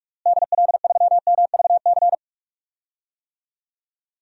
Morse code, heard as a keyed tone, about 43 wpm, 700 Hz. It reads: DL3MVC
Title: DXCC-100 Call Signs - 6 Characters - Repeated 3x - 45wpm